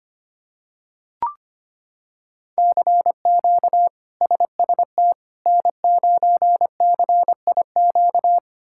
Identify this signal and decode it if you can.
Morse code, heard as a keyed tone, about 25 wpm, 700 Hz. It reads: CQ SST N9CIQ
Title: K1USN SST QSO Practice - 25wpm